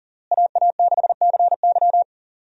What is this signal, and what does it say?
Morse code, heard as a keyed tone, about 40 wpm, 700 Hz. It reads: AA6CY
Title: USA Call Signs - 5 Characters - Repeated 3x - 40wpm